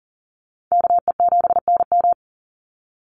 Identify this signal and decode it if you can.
Morse code, heard as a keyed tone, about 40 wpm, 690 Hz. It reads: KE7NM
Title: USA Call Signs 40wpm